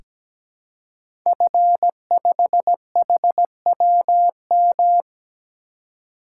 Morse code, heard as a keyed tone, about 17 wpm, 700 Hz. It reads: F5HWM
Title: Code Groups - 5 Letter:Numbers - 17wpm